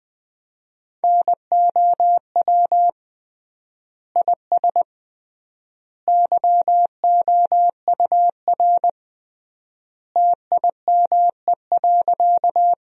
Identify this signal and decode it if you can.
Morse code, heard as a keyed tone, about 21 wpm, 700 Hz. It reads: NOW IS YOUR TIME.
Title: 3x Word-Spacing - Sentences from Top 100 Words - 20wpm